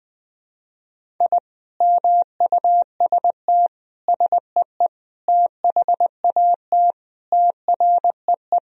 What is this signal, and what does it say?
Morse code, heard as a keyed tone, about 20 wpm, 700 Hz. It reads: I MUST SEE THAT TREE
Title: Sentences from Top 1000 Words - Review Part 1 - 20wpm